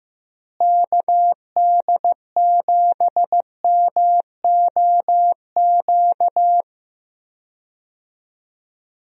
Morse code, heard as a keyed tone, about 15 wpm, 700 Hz. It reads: KD7MOQ
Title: USA Call Signs - 6 Characters - Repeated 3x - 15wpm